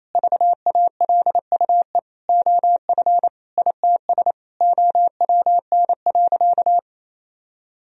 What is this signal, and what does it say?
Morse code, heard as a keyed tone, about 28 wpm, 700 Hz. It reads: VALUE OF ITS OWN.
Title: Sentences from Top 1000 Words - Review Part 4 - 28wpm